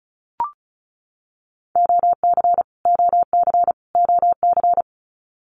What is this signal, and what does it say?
Morse code, heard as a keyed tone, about 34 wpm, 700 Hz. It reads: OC OC OC
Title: Abbreviations Repeated 3x - 35wpm